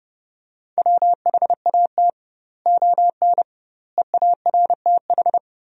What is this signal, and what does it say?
Morse code, heard as a keyed tone, about 32 wpm, 700 Hz. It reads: WHAT ON EARTH
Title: 2x Word-Spacing - Sentences from Top 200 Words - 30wpm